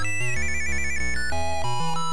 There is no code.